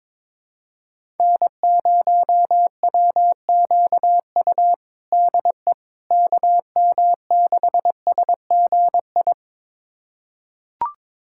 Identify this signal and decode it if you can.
Morse code, heard as a keyed tone, about 21 wpm, 700 Hz. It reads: N0WQU DE KM6SGI
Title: USA Callsign DE Callsign 22wpm